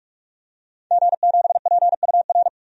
Morse code, heard as a keyed tone, about 46 wpm, 700 Hz. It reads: G7PUR
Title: DXCC-100 Call Signs - 5 Characters - Repeated 3x - 45wpm